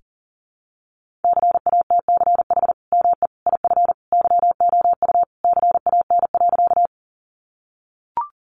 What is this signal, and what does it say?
Morse code, heard as a keyed tone, about 37 wpm, 700 Hz. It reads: CATCH ME IF YOU CAN.
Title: Sentences from Top 1500 Words 40wpm